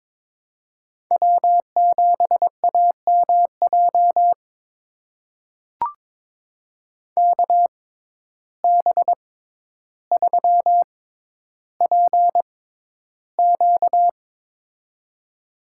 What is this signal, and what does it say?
Morse code, heard as a keyed tone, about 22 wpm, spaced as for 11 wpm, 700 Hz. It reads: W7AMJ KB3PQ
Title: USA Call Signs - 5 Characters - 6x Character Spacing - 22wpm